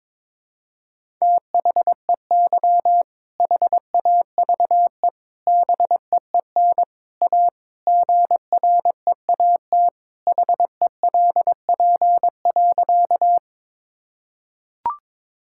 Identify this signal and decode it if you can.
Morse code, heard as a keyed tone, about 22 wpm, 700 Hz. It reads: THEY HAVE BEEN A GREAT HELP.
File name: practice / 22wpm / Sentences with 3-Word Phrases - 22wpm